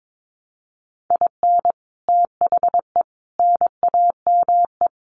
Morse code, heard as a keyed tone, about 22 wpm, 700 Hz. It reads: IN THE NAME
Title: Sets of 5 Words Encore 22wpm